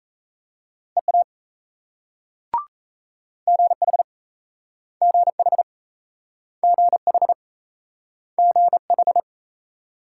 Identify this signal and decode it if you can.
Morse code, heard as a keyed tone, about 36 wpm, 700 Hz. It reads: EA GH GH GH GH